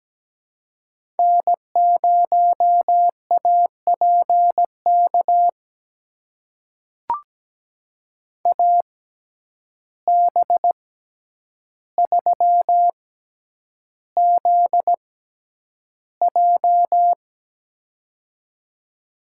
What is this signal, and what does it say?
Morse code, heard as a keyed tone, about 17 wpm, spaced as for 9 wpm, 700 Hz. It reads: N0APK AB3ZJ
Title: USA Call Signs - 5 Characters - 6x Character Spacing - 17wpm